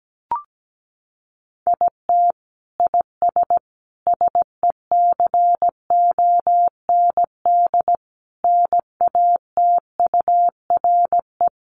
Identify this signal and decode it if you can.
Morse code, heard as a keyed tone, about 17 wpm, 700 Hz. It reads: IT IS SECOND NATURE
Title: Sentences from Top 600 Words 17wpm